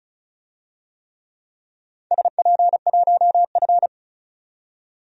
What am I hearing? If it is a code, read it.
Morse code, heard as a keyed tone, about 34 wpm, 700 Hz. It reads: SP1F
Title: DXCC-100 Call Signs - 4 Characters - 35wpm